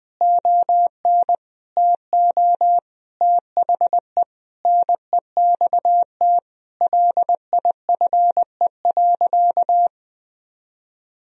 Morse code, heard as a keyed tone, about 20 wpm, 700 Hz. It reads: ON TO THE NEXT LIFE.
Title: Sentences from Top 1000 Words - Review Part 1 - 20wpm